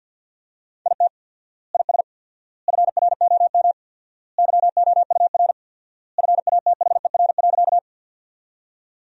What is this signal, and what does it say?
Morse code, heard as a keyed tone, about 47 wpm, 700 Hz. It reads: IT IS FROM YOUR FATHER.
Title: 4x Word-Spacing - Sentences from Top 300 Words - 50wpm